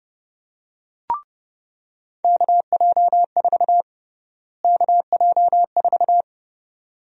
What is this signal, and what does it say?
Morse code, heard as a keyed tone, about 30 wpm, 700 Hz. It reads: KJ4 KJ4
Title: USA Call Sign Prefixes - Repeated 2x - 30wpm